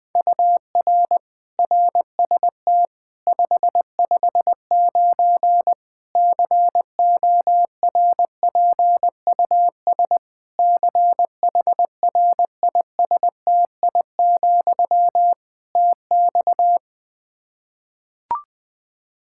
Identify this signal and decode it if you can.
Morse code, heard as a keyed tone, about 20 wpm, 700 Hz. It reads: UR RST 559 CORPUS CHRISTI, TX
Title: Practice QSO - Exchange 1 - Beginning - 20wpm